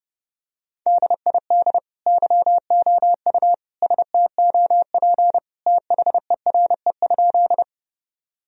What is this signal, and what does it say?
Morse code, heard as a keyed tone, about 30 wpm, 700 Hz. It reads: DID YOU STOP THERE?
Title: Sentences from Top 1000 Words - Review Part 1 - 30wpm